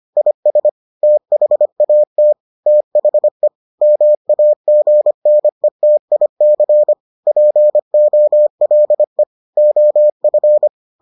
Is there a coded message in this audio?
Yes: Morse code from 0.2 to 10.7 s, heard as a keyed tone, about 25 wpm, 600 Hz. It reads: IS THAT THE MAGNETIC POLE OF